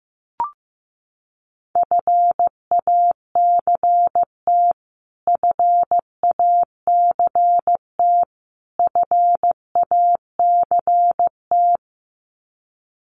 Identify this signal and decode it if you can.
Morse code, heard as a keyed tone, about 15 wpm, 700 Hz. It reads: FACT FACT FACT
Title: Top 500 Words Repeated 15wpm